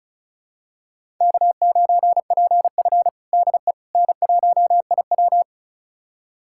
Morse code, heard as a keyed tone, about 36 wpm, 700 Hz. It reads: K9PF DE N1IW
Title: USA Callsign DE Callsign - 4 Characters - 35wpm